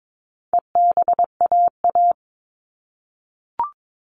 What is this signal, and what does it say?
Morse code, heard as a keyed tone, about 22 wpm, 700 Hz. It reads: EBAA
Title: All Cut Numbers - 4-Digit 22wpm